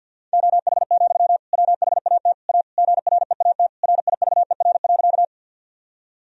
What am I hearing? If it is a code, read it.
Morse code, heard as a keyed tone, about 47 wpm, 700 Hz. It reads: OH, WHAT A GREAT RIVER.
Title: Sentences from Top 1000 Words - Review Part 1 - 50wpm